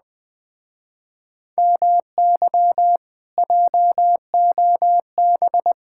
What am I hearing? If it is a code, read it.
Morse code, heard as a keyed tone, about 20 wpm, 700 Hz. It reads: MY JOB
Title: Speed-Racing Common QSO Phrases - 17wpm